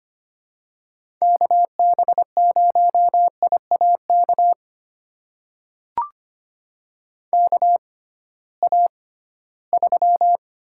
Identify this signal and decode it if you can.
Morse code, heard as a keyed tone, about 25 wpm, spaced as for 16 wpm, 700 Hz. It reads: KB0IAK KA3
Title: USA Call Signs - 6x Character Spacing - 25wpm